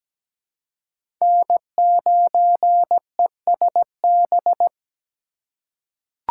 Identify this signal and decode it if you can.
Morse code, heard as a keyed tone, about 17 wpm, 700 Hz. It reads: N9ESB
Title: USA Call Signs - 5 Characters - 4x Character Spacing - 17wpm